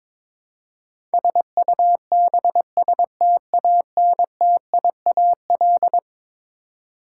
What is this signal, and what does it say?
Morse code, heard as a keyed tone, about 22 wpm, 700 Hz. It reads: SUBSTANTIAL
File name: practice / 22wpm / Eleven Letter Words - 22wpm